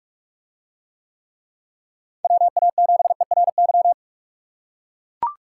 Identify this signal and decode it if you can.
Morse code, heard as a keyed tone, about 41 wpm, 700 Hz. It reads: WA7ERY